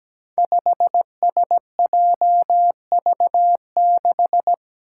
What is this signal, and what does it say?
Morse code, heard as a keyed tone, about 17 wpm, 700 Hz. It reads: 5SJV6
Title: Five Challenging Letter-Number Combinations 17wpm